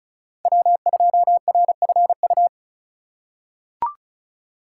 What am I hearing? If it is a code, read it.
Morse code, heard as a keyed tone, about 36 wpm, 700 Hz. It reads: W2RFU
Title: USA Call Signs - 5 Characters - 4x Character Spacing - 35wpm